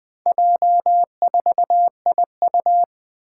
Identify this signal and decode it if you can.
Morse code, heard as a keyed tone, about 20 wpm, 700 Hz. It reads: J4IU
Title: Four Challenging Letter-Number Combinations 20wpm